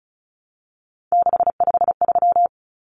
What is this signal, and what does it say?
Morse code, heard as a keyed tone, about 34 wpm, 700 Hz. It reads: 653
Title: Numbers - 3-Digit 35wpm